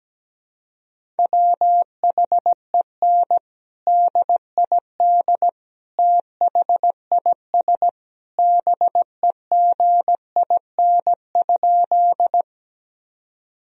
Morse code, heard as a keyed tone, about 17 wpm, 700 Hz. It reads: WHEN DID THIS BEGIN?
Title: Sentences from Top 1000 Words - Review Part 1 - 17wpm